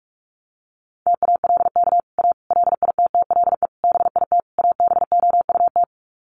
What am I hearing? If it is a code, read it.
Morse code, heard as a keyed tone, about 47 wpm, 700 Hz. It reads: TALK A LITTLE BIT ABOUT
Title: Sets of 5 Words Encore 45wpm